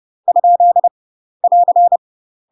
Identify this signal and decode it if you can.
Morse code, heard as a keyed tone, about 30 wpm, 700 Hz. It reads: ? <AR>
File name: UK MASTER QSOs TRAINING / 30WPM / UK MASTER QSOs 0010